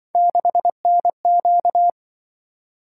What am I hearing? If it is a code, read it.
Morse code, heard as a keyed tone, about 24 wpm, 700 Hz. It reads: 6NQ
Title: Speed-Racing - Three Letter-Number Combinations - 20wpm